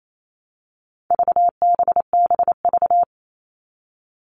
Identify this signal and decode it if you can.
Morse code, heard as a keyed tone, about 28 wpm, 700 Hz. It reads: VBBV
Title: All Cut Numbers - 4-Digit 28wpm